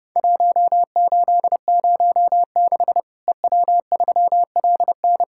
Morse code, heard as a keyed tone, about 31 wpm, 700 Hz. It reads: 1806 EW3LN
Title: Worked All Europe DX Contest - QTC Exchange - 30wpm